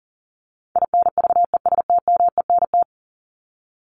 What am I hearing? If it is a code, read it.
Morse code, heard as a keyed tone, about 40 wpm, 700 Hz. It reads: INVESTMENT
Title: Top 501-750 Words 40wpm